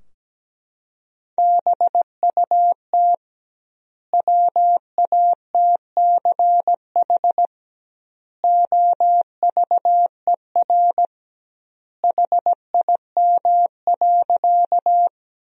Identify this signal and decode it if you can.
Morse code, heard as a keyed tone, about 17 wpm, 700 Hz. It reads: BUT WATCH OVER HIM.
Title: 2x Word-Spacing - Sentences from Top 300 Words - 17wpm